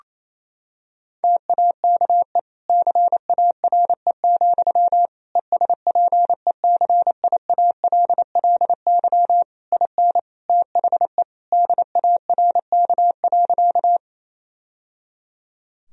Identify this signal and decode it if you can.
Morse code, heard as a keyed tone, about 28 wpm, 700 Hz. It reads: TAKE CARE, ESPECIALLY IN THE DARK.
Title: Sentences from Top 1000 Words - Review Part 4 - 28wpm